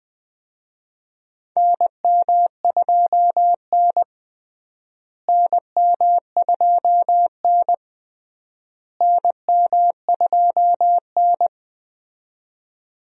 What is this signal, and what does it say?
Morse code, heard as a keyed tone, about 20 wpm, 700 Hz. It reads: NM2N NM2N NM2N
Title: USA Call Signs - 4 Characters - Repeated 3x - 20wpm